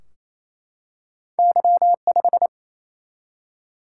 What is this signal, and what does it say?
Morse code, heard as a keyed tone, about 27 wpm, 700 Hz. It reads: Y 5
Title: Single-Letter Single-Number 28wpm